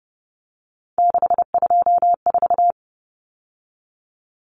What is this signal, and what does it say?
Morse code, heard as a keyed tone, about 30 wpm, 700 Hz. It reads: 624
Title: Numbers - 3-Digit 30wpm